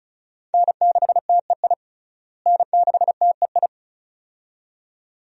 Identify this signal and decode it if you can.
Morse code, heard as a keyed tone, about 34 wpm, 700 Hz. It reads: N6TEI N6TEI
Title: USA Call Signs - 5 Characters - Repeated 2x - 35wpm